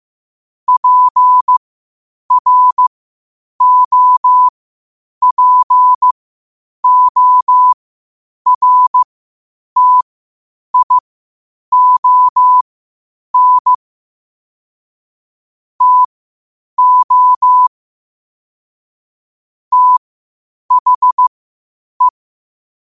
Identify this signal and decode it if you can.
Morse code, heard as a keyed tone, about 15 wpm, spaced as for 8 wpm, 1000 Hz. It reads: PROPORTION TO THE
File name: qotdfiles / 8wpm